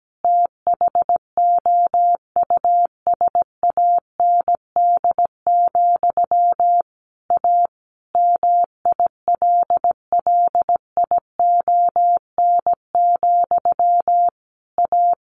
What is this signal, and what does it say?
Morse code, heard as a keyed tone, about 17 wpm, 700 Hz. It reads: THOUSAND, A MILLION, A
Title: A Sound Of Thunder 17wpm